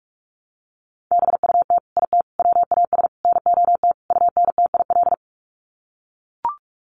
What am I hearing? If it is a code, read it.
Morse code, heard as a keyed tone, about 44 wpm, 700 Hz. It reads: BUT IT WAS NOT UNTIL
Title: Sets of 5 Words Encore2 45wpm